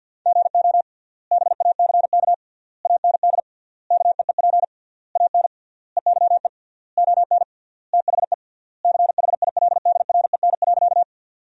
Morse code, heard as a keyed tone, about 50 wpm, 700 Hz. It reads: GO BACK AND KEEP AN EYE ON THE CHILDREN.
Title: 3x Word-Spacing - Sentences from Top 300 Words - 50wpm